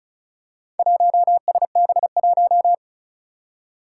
Morse code, heard as a keyed tone, about 36 wpm, 700 Hz. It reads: 1SB1
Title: Four Challenging Letter-Number Combinations 35wpm